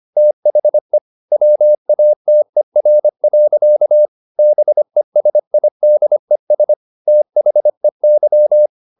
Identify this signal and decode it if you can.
Morse code, heard as a keyed tone, about 25 wpm, 600 Hz. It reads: THE WATER. BESIDES THEY